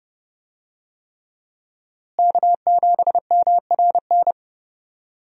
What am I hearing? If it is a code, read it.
Morse code, heard as a keyed tone, about 30 wpm, 700 Hz. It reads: K7MRN
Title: USA Call Signs - 5 Characters - Repeated 3x - 30wpm